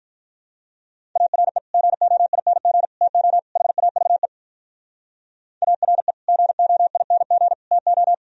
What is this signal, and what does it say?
Morse code, heard as a keyed tone, about 52 wpm, 700 Hz. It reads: ARE GOING TO HAVE ARE GOING TO